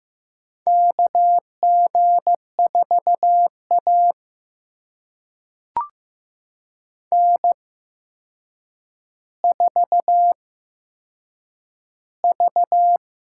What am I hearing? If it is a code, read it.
Morse code, heard as a keyed tone, about 15 wpm, spaced as for 7 wpm, 700 Hz. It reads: KG4A N4V
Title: USA Call Signs - 4 Characters - 8x Character Spacing - 15wpm